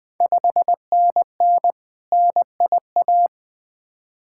Morse code, heard as a keyed tone, about 20 wpm, 700 Hz. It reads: 5NN NIA
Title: Ontario QSO Party Exchange 20wpm